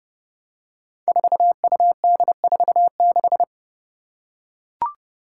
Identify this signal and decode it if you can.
Morse code, heard as a keyed tone, about 30 wpm, 700 Hz. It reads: 4UD46
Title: Five Challenging Letter-Number Combinations 30wpm